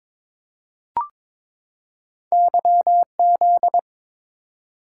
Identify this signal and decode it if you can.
Morse code, heard as a keyed tone, about 22 wpm, 700 Hz. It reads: YZ
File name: practice / 22wpm / Two Letter-Number Combinations - 22wpm